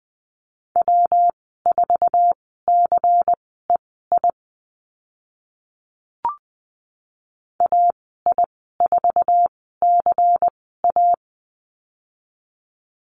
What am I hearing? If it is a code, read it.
Morse code, heard as a keyed tone, about 20 wpm, spaced as for 14 wpm, 700 Hz. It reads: W4CEI AI4CA
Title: 140 Lesson 23g - Letter f - Call Signs 20wpm